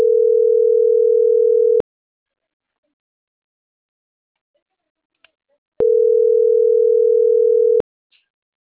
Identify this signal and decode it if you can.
Ringback tone (North America), 440 + 480 Hz.